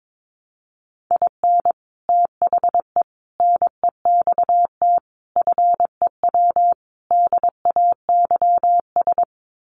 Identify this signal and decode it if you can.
Morse code, heard as a keyed tone, about 22 wpm, 700 Hz. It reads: IN THE NEXT FEW DAYS
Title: Sets of 5 Words Encore2 22wpm